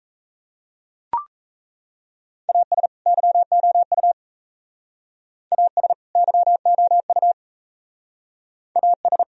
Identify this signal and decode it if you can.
Morse code, heard as a keyed tone, about 34 wpm, 700 Hz. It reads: AS YOU AS YOU AS
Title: Speed-Racing - Sets of 2 Words 28wpm